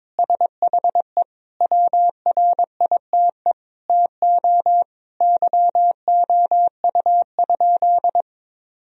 Morse code, heard as a keyed tone, about 22 wpm, 700 Hz. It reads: SHE WRITE TO YOU?
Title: Sentences from Top 1000 Words - Review Part 1 - 22wpm